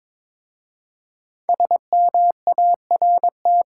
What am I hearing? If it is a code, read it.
Morse code, heard as a keyed tone, about 22 wpm, 690 Hz. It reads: SMART
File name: Difficult Words with Hints - 22wpm